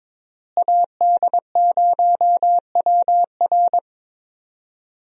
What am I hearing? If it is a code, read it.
Morse code, heard as a keyed tone, about 22 wpm, 700 Hz. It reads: AD0WR
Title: USA Call Signs - 5 Characters - Repeated 3x - 22wpm